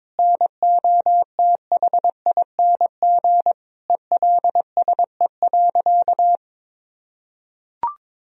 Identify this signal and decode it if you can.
Morse code, heard as a keyed tone, about 22 wpm, 700 Hz. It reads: NOTHING ELSE.
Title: Sentences from Top 1000 Words - Review Part 1 - 22wpm